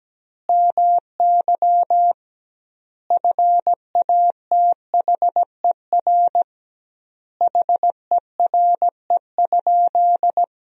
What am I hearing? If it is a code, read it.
Morse code, heard as a keyed tone, about 17 wpm, 700 Hz. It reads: MY FATHER HERE?
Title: 2x Word-Spacing - Sentences from Top 300 Words - 17wpm